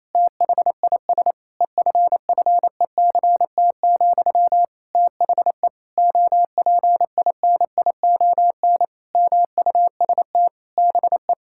Morse code, heard as a keyed tone, about 28 wpm, 700 Hz. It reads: THIS EFFECT, THE OPINION MUST BE
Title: Sentences from Top 1600 Words 28wpm